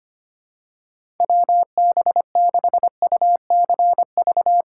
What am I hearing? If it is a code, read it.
Morse code, heard as a keyed tone, about 25 wpm, 700 Hz. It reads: WB6UCV
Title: USA Call Signs - 6 Characters - 4x Character Spacing - 25wpm